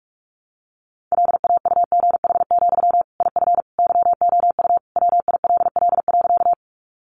Morse code, heard as a keyed tone, about 44 wpm, 700 Hz. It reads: LAUGH, IF YOU WILL.
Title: Sentences from Top 500 Words 45wpm